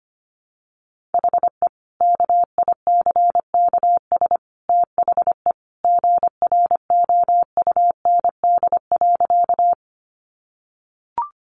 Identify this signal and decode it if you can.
Morse code, heard as a keyed tone, about 25 wpm, 700 Hz. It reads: HE KICKS THE GROUND.
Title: Additional Easy Phrases 25wpm